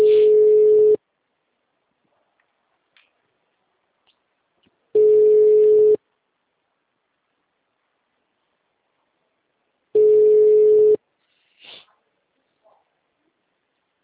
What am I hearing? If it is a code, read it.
Ringback tone (Europe, 425 Hz).
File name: out- -2015- - - .8217